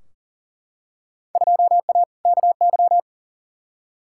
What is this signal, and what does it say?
Morse code, heard as a keyed tone, about 40 wpm, 700 Hz. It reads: 2A KY
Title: Field Day Exchanges - 40wpm